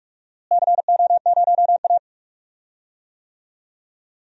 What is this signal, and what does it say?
Morse code, heard as a keyed tone, about 42 wpm, 700 Hz. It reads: CO0A
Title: DXCC-100 Call Signs - 4 Characters - Repeated 2x - 45wpm